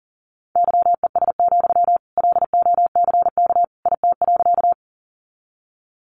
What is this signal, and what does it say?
Morse code, heard as a keyed tone, about 40 wpm, 700 Hz. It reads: YES, LOCK IT.
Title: Sentences from Top 1300 Words 40wpm